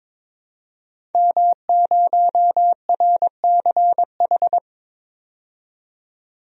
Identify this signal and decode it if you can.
Morse code, heard as a keyed tone, about 22 wpm, 700 Hz. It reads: M0RCH
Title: DXCC-100 Call Signs - 5 Characters - 22wpm